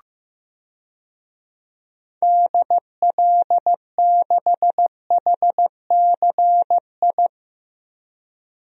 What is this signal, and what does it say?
Morse code, heard as a keyed tone, about 16 wpm, 700 Hz. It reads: DL6HCI
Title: DXCC-100 Call Signs - 6 Characters - Repeated 3x - 15wpm